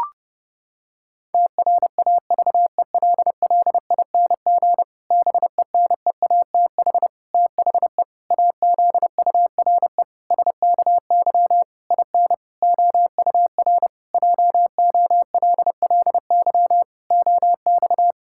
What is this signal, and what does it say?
Morse code, heard as a keyed tone, about 32 wpm, 700 Hz. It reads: TRAVELLING BENEATH THE AZURE SKY IN OUR JOLLY OX
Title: Pangram Challenge v2 - 30wpm